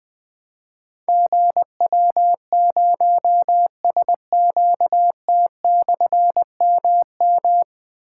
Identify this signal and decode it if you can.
Morse code, heard as a keyed tone, about 20 wpm, 700 Hz. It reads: GW0SQT/MM
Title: Difficult Call Signs 20wpm